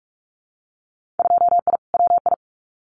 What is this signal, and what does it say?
Morse code, heard as a keyed tone, about 46 wpm, 700 Hz. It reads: 2I WI
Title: Winter Field Day Exchanges 45wpm